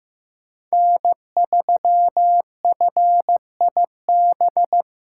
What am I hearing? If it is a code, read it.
Morse code, heard as a keyed tone, about 18 wpm, 700 Hz. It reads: N3FIB
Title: Speed Racing - USA 1x3 Call Signs 15wpm